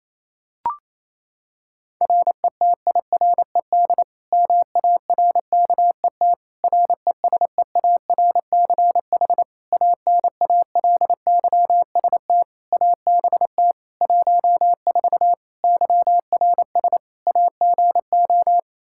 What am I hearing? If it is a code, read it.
Morse code, heard as a keyed tone, about 28 wpm, 700 Hz. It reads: RETIRED MARKET RESEARCH ANALYST ABT 14 YRS AGO
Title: Practice QSO - Exchange 3 - More Info - 28wpm